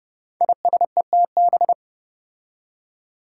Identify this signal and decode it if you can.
Morse code, heard as a keyed tone, about 30 wpm, 700 Hz. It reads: ISETB
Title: Five Challenging Letter-Number Combinations 30wpm